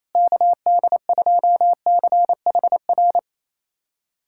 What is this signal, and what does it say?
Morse code, heard as a keyed tone, about 28 wpm, 700 Hz. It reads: KD2CHR
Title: USA Callsign DE Callsign - 6 Characters - 28wpm